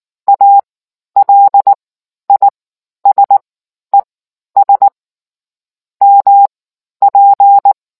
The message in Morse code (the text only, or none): ALISES MP